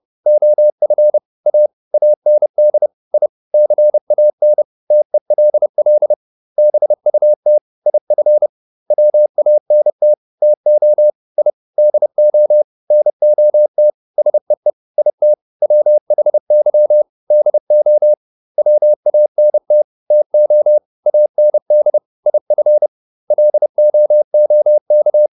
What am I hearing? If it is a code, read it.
Morse code, heard as a keyed tone, about 30 wpm, 600 Hz. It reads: OF A AND I CAN TELL BUT IF WANT TO I DO NOT SEE IT WHY DO WANT TO AND IF LOOK